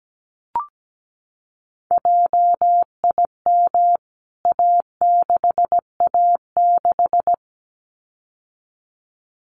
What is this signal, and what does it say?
Morse code, heard as a keyed tone, about 17 wpm, 700 Hz. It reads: JIM A6A6
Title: CWT Exchange 17wpm